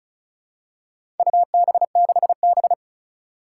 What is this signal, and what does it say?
Morse code, heard as a keyed tone, about 36 wpm, 700 Hz. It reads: UB6B
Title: Four Challenging Letter-Number Combinations 35wpm